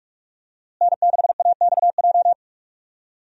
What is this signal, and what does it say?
Morse code, heard as a keyed tone, about 48 wpm, 700 Hz. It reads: N6AXJ
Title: USA Call Signs - 5 Characters - Repeated 2x - 45wpm